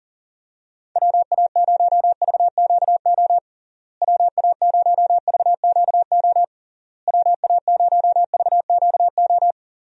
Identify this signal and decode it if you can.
Morse code, heard as a keyed tone, about 37 wpm, 700 Hz. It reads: WA0VQO WA0VQO WA0VQO
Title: USA Call Signs - 6 Characters - Repeated 3x - 40wpm